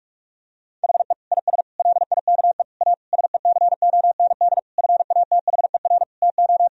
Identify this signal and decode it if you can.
Morse code, heard as a keyed tone, about 41 wpm, 700 Hz. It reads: HE IS LIKE A SECOND FATHER TO
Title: Sentences from Top 1000 Words - Review Part 1 - 45wpm